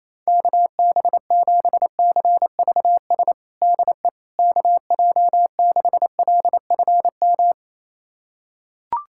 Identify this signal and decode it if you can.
Morse code, heard as a keyed tone, about 28 wpm, 700 Hz. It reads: KB7CVS DE KJ6LFM
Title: USA Callsign DE Callsign - 6 Characters - 28wpm